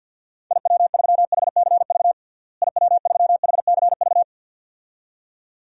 Morse code, heard as a keyed tone, about 53 wpm, 700 Hz. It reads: IW3HCV IW3HCV
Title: DXCC-100 Call Signs - 6 Characters - Repeated 2x - 50wpm